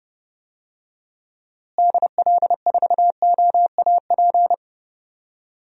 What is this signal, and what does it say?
Morse code, heard as a keyed tone, about 30 wpm, 700 Hz. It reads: DL4OAP
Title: USA Call Signs - 6 Characters - Repeated 3x - 30wpm